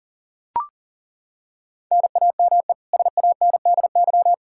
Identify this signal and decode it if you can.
Morse code, heard as a keyed tone, about 40 wpm, 700 Hz. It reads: NAME SANDY
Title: K1USN SST QSO Practice 40wpm